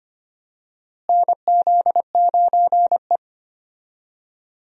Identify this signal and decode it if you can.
Morse code, heard as a keyed tone, about 25 wpm, 700 Hz. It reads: NZ9E
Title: USA Call Signs - 4 Characters - Repeated 3x - 25wpm